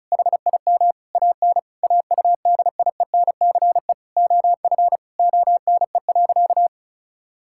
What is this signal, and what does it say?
Morse code, heard as a keyed tone, about 36 wpm, 690 Hz. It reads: HIM AN AUDIENCE OF ONE.
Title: Sentences from Top 1000 Words - Review Part 5 - 35wpm